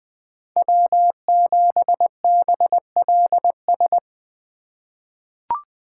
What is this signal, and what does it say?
Morse code, heard as a keyed tone, about 20 wpm, 700 Hz. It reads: W7BLS
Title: USA Call Signs - 5 Characters - 3x Character Spacing - 20wpm